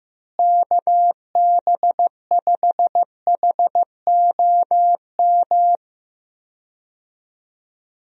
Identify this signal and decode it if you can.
Morse code, heard as a keyed tone, about 16 wpm, 700 Hz. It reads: KB5HOM
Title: USA Call Signs - 6 Characters - Repeated 3x - 15wpm